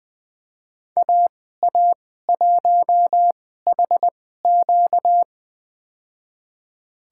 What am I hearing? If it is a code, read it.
Morse code, heard as a keyed tone, about 20 wpm, spaced as for 14 wpm, 700 Hz. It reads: AA1HQ
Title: USA Call Signs - 2x Character Spacing - 20wpm